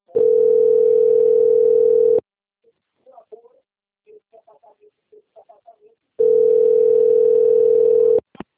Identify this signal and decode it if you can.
Ringback tone (North America), 440 + 480 Hz.
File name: out- - - - - .16444